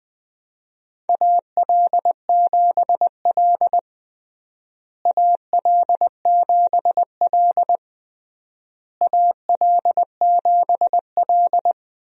Morse code, heard as a keyed tone, about 20 wpm, 700 Hz. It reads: AL7L AL7L AL7L
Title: USA Call Signs - 4 Characters - Repeated 3x - 20wpm